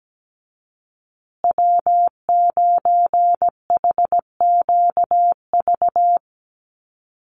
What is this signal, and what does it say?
Morse code, heard as a keyed tone, about 17 wpm, 700 Hz. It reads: W9HQV
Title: USA Call Signs Repeated 2x 17wpm